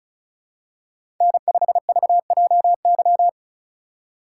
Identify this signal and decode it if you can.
Morse code, heard as a keyed tone, about 36 wpm, 700 Hz. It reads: N5VJY
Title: USA Call Signs - 3x Character Spacing - 35wpm